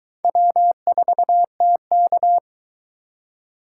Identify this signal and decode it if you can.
Morse code, heard as a keyed tone, about 23 wpm, 700 Hz. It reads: W4TK